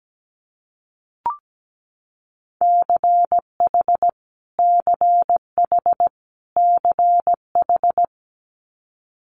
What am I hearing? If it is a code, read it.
Morse code, heard as a keyed tone, about 17 wpm, 700 Hz. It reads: CH CH CH
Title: Multi-letter Phonemes Repeated 3x 17wpm